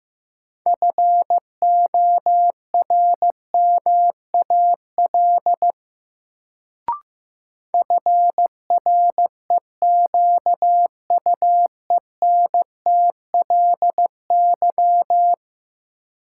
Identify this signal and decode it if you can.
Morse code, heard as a keyed tone, about 15 wpm, 700 Hz. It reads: FORMAL FREQUENTLY
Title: Sentences from Top 1300 Words - New Word Drill Alt - 15wpm